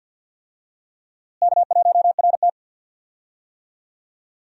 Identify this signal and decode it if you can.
Morse code, heard as a keyed tone, about 53 wpm, 700 Hz. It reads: K1RT
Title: USA Call Signs - 4 Characters - Repeated 3x - 50wpm